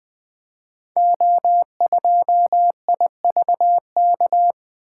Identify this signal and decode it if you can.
Morse code, heard as a keyed tone, about 20 wpm, 700 Hz. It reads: O2IVK
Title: Code Groups - 5 Letter:Numbers - 20wpm